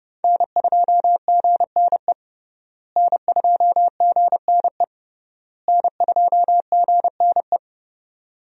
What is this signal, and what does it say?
Morse code, heard as a keyed tone, about 32 wpm, 700 Hz. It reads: N2GNE N2GNE N2GNE
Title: USA Call Signs - 5 Characters - Repeated 3x - 30wpm